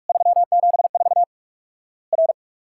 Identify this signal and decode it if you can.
Morse code, heard as a keyed tone, about 46 wpm, 700 Hz. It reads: 374 R
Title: Worked All Europe DX Contest - QTC Exchange - 45wpm